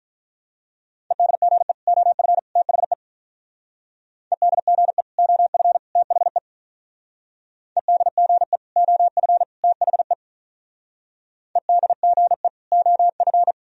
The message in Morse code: EDGE OF THE EDGE OF THE EDGE OF THE EDGE OF

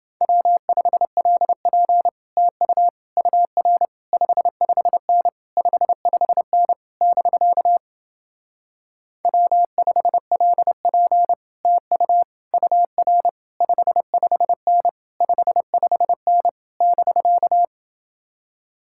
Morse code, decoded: W5LP TU UR 55N 55N <BK> W5LP TU UR 55N 55N <BK>

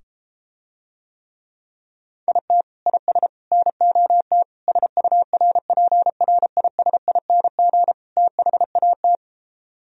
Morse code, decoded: IT IS NOT SURPRISING THAT

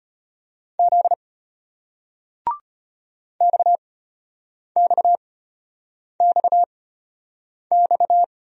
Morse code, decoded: Z XXXX

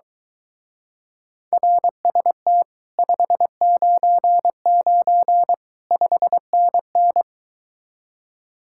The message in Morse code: RST 599 5NN

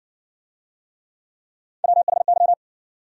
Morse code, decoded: WHY

